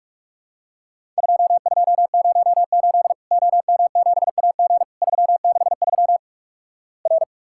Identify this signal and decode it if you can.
Morse code, heard as a keyed tone, about 46 wpm, 700 Hz. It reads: 2208 OM7AG 363 R
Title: Worked All Europe DX Contest - QTC Exchange - 45wpm